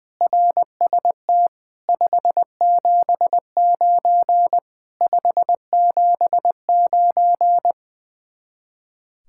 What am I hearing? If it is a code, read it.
Morse code, heard as a keyed tone, about 20 wpm, 700 Hz. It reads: RST 579 579
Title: Speed-Racing Common QSO Phrases Encore - 20wpm